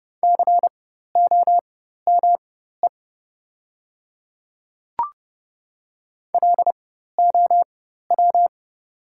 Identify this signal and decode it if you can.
Morse code, heard as a keyed tone, about 30 wpm, spaced as for 14 wpm, 690 Hz. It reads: COME LOW
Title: 4x Character Spacing - Top 200 Words - 30wpm